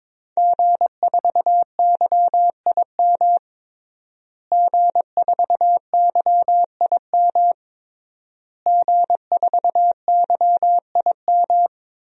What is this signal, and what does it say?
Morse code, heard as a keyed tone, about 22 wpm, 700 Hz. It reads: G4YIM G4YIM G4YIM
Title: DXCC-100 Call Signs - 5 Characters - Repeated 3x - 22wpm